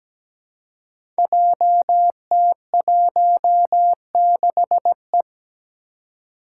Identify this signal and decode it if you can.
Morse code, heard as a keyed tone, about 17 wpm, 700 Hz. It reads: JT16E
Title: Five Challenging Letter-Number Combinations 17wpm